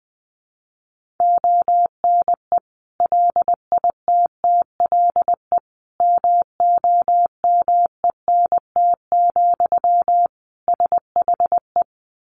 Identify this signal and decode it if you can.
Morse code, heard as a keyed tone, about 20 wpm, 700 Hz. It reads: ONE LITTLE MOMENT, SHE
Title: Sentences from Top 1000 Words 20wpm